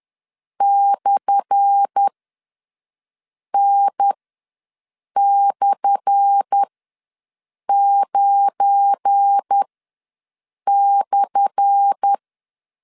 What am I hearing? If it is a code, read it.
Morse code, heard as a keyed tone, about 10 wpm, spaced as for 6 wpm, 800 Hz. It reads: /N/9/
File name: LEZIONI / 10 SIGNS / 10A SLASH ws 10wpm 4121